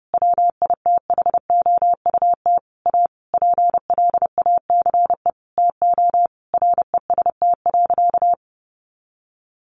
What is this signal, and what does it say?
Morse code, heard as a keyed tone, about 30 wpm, 700 Hz. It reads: WITHOUT A PLACE TO REST.
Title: Sentences from Top 600 Words 30wpm